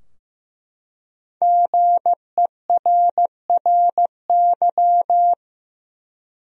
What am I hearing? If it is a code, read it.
Morse code, heard as a keyed tone, about 15 wpm, 700 Hz. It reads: GERRY
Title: USA Common Names - 15wpm